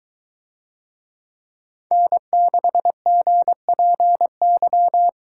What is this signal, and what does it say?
Morse code, heard as a keyed tone, about 22 wpm, 700 Hz. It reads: N6GPY